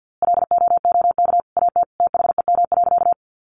Morse code, heard as a keyed tone, about 50 wpm, 700 Hz. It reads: LOOK AT THEM.
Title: A Sound Of Thunder 50wpm